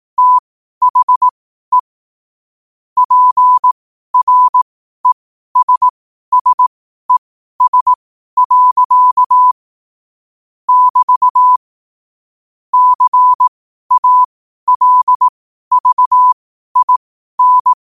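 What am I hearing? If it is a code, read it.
Morse code, heard as a keyed tone, about 18 wpm, spaced as for 12 wpm, 1000 Hz. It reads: THE PRESSES. <BT> CALVIN
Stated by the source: Morse code quotes at 12 words per minute.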